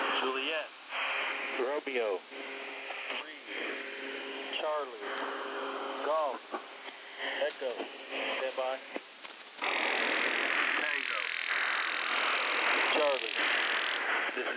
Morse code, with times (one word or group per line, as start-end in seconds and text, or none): none